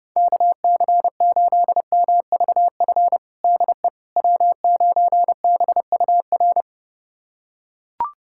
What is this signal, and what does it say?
Morse code, heard as a keyed tone, about 32 wpm, 700 Hz. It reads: KC8MVF DE W9BUR
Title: USA Callsign DE Callsign 30wpm